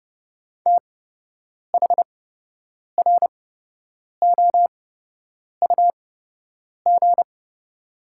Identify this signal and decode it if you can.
Morse code, heard as a keyed tone, about 30 wpm, spaced as for 8 wpm, 700 Hz. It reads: THROUG
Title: 8x Character Spacing - Top 200 Words - 30wpm